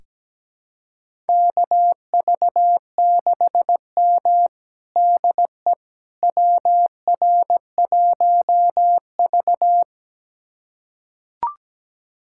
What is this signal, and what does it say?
Morse code, heard as a keyed tone, about 17 wpm, 700 Hz. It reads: KV6M DE WR1V
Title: USA Callsign DE Callsign - 4 Characters - 17wpm